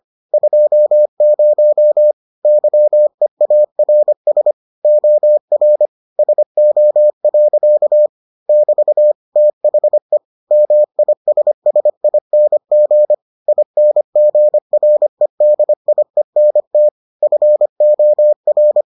Morse code, heard as a keyed tone, about 25 wpm, 600 Hz. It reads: 20 YEARS OR SO. <BT> THE MISSING INGREDIENT FOR